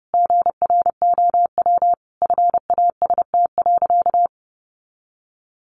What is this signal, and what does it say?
Morse code, heard as a keyed tone, about 30 wpm, 700 Hz. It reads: GROW FAST.
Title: Sentences from Top 1300 Words 30wpm